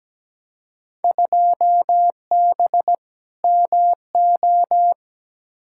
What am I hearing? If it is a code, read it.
Morse code, heard as a keyed tone, about 17 wpm, 700 Hz. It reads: 2B MO
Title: Field Day Exchanges - 17wpm